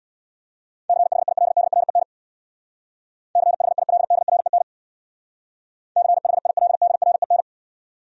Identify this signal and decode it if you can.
Morse code, heard as a keyed tone, about 66 wpm, 700 Hz. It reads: CHILDREN CHILDREN CHILDREN